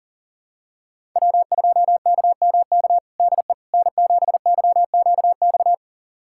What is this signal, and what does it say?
Morse code, heard as a keyed tone, about 40 wpm, 700 Hz. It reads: W2KMK DE N7YQX
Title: USA Callsign DE Callsign - 5 Characters - 40wpm